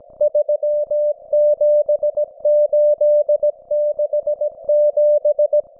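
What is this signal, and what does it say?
Morse code, heard as a keyed tone, about 17 wpm, 590 Hz. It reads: 37867
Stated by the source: Russian military Morse station, M32, RDL, on 5268 kHz.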